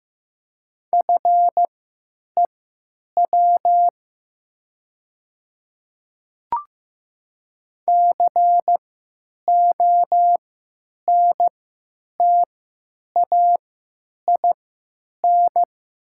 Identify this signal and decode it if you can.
Morse code, heard as a keyed tone, about 15 wpm, spaced as for 9 wpm, 700 Hz. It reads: FEW CONTAIN
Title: 3x Character Spacing - Top 500 Words - 15wpm